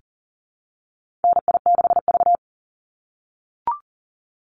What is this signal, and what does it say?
Morse code, heard as a keyed tone, about 41 wpm, 700 Hz. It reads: NI6V
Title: Call Signs - 40wpm